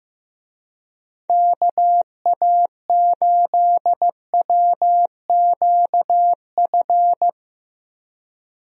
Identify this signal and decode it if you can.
Morse code, heard as a keyed tone, about 15 wpm, 700 Hz. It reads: KA8WQF
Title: USA Call Signs - 6 Characters - 15wpm